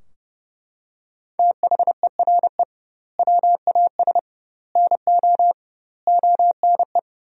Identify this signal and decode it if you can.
Morse code, heard as a keyed tone, about 32 wpm, 700 Hz. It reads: THERE WAS NO ONE
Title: 2x Word-Spacing - Sentences from Top 200 Words - 30wpm